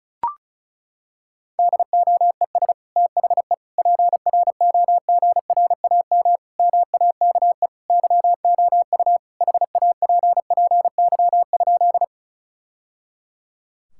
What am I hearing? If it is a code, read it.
Morse code, heard as a keyed tone, about 34 wpm, 700 Hz. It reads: DOES THE PROGRAM MAKE YOU HAPPY?
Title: Sentences from Top 1000 Words - Review Part 5 - 35wpm